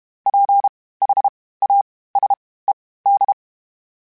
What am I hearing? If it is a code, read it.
Morse code, heard as a keyed tone, about 32 wpm, spaced as for 18 wpm, 800 Hz. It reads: PHASED
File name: (spaced) / FromSMs spaced mp3s / Antennas sp@30